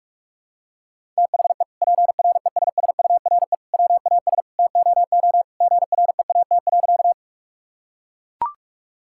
Morse code, heard as a keyed tone, about 41 wpm, 700 Hz. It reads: THE PRESSURE WAS TOO GREAT.
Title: Sentences from Top 1000 Words - Review Part 6 - 45wpm